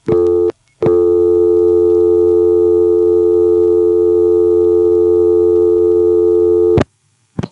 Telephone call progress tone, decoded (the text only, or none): dial tone